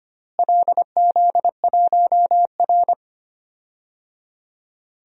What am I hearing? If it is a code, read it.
Morse code, heard as a keyed tone, about 25 wpm, 700 Hz. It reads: LZ1R
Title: DXCC-100 Call Signs - 4 Characters - Repeated 3x - 25wpm